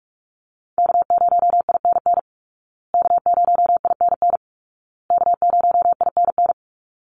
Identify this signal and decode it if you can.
Morse code, heard as a keyed tone, about 46 wpm, 700 Hz. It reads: K0INN K0INN K0INN
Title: Call Signs Repeated 2x - 45wpm